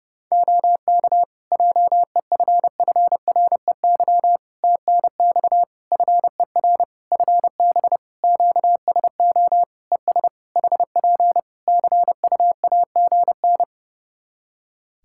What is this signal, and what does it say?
Morse code, heard as a keyed tone, about 30 wpm, 700 Hz. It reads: OK JEFFREY TNX FER FB QSO ES HP CUAGN
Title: Practice QSO - Exchange 4 - Ending - 30wpm